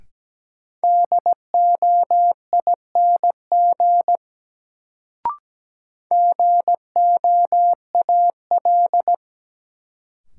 Morse code, heard as a keyed tone, about 17 wpm, 700 Hz. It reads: DOING GOAL
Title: Sentences from Top 1400 Words - New Word Drill Alt - 17wpm